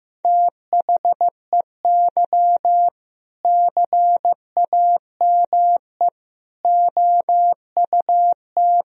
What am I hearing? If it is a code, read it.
Morse code, heard as a keyed tone, about 16 wpm, 700 Hz. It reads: THEY CAME OUT
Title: Sentences from Top 1000 Words - Review Part 3 - 15wpm